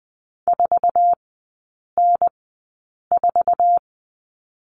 Morse code, heard as a keyed tone, about 20 wpm, spaced as for 8 wpm, 700 Hz. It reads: 4N4
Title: 047 Lesson 9e - Number 4 - Three Characters 20wpm